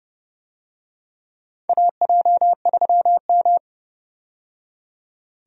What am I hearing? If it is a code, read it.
Morse code, heard as a keyed tone, about 30 wpm, 700 Hz. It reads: AJ3M
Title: USA Call Signs - 4 Characters - 30wpm